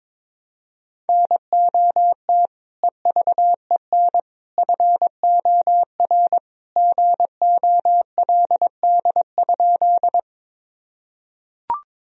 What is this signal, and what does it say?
Morse code, heard as a keyed tone, about 22 wpm, 700 Hz. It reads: NOT EVEN FOR GOLD?
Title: Sentences from Top 1000 Words - Review Part 3 - 22wpm